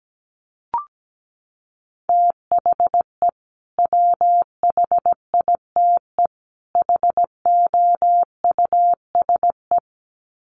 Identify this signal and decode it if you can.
Morse code, heard as a keyed tone, about 17 wpm, 700 Hz. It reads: THE WHITE HOUSE
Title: Sets of 3 Words 17wpm